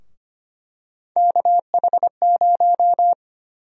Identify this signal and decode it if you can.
Morse code, heard as a keyed tone, about 25 wpm, 700 Hz. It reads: KH0
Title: USA Call Sign Prefixes - 25wpm